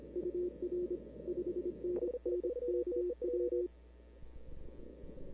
Morse code, heard as a keyed tone, about 26 wpm, 360 Hz. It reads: UR 5NN KW
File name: CW / 2005-k5tr arrl cw dx -00:00